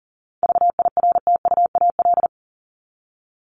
Morse code, heard as a keyed tone, about 40 wpm, 700 Hz. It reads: VIRTUAL
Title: LY Suffix - Top 100 Words with Base Words 40wpm